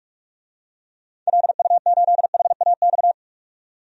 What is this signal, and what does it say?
Morse code, heard as a keyed tone, about 47 wpm, 700 Hz. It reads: LU8HAX
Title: DXCC-100 Call Signs - 6 Characters - 45wpm